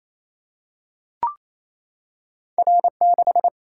Morse code, heard as a keyed tone, about 28 wpm, 700 Hz. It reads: R6
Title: Single-Letter Single-Number 28wpm